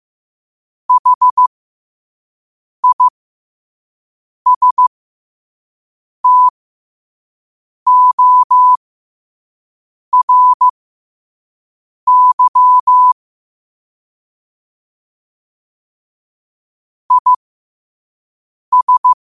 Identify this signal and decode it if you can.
Morse code, heard as a keyed tone, about 15 wpm, spaced as for 5 wpm, 1000 Hz. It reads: HISTORY IS
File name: qotdfiles / 5wpm